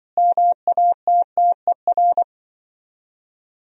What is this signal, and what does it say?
Morse code, heard as a keyed tone, about 25 wpm, 700 Hz. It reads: MATTER
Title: Speed-Racing - Sets of 5 Words 20wpm